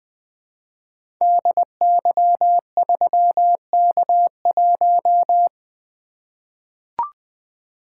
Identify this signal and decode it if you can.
Morse code, heard as a keyed tone, about 20 wpm, 700 Hz. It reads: DY3K1
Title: Code Groups - 5 Letter:Numbers - 20wpm